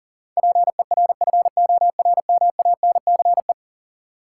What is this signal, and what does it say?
Morse code, heard as a keyed tone, about 40 wpm, 700 Hz. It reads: PERFORMANCE
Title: Sentences from Top 1000 Words - New Word Drill - 40wpm